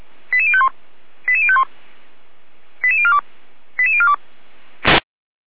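Five-tone selective call (ZVEI), heard as quick runs of five tones, tones of about 70 ms. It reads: 88051 88051 88042 88042
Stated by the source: ZVEI five-tone selective call received by SDR, from the feuerwehr-orlamuende (Orlamünde fire brigade) server.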